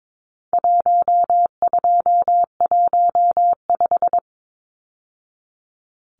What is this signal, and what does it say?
Morse code, heard as a keyed tone, about 22 wpm, 700 Hz. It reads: 1215
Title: Numbers - 4-Digit 22wpm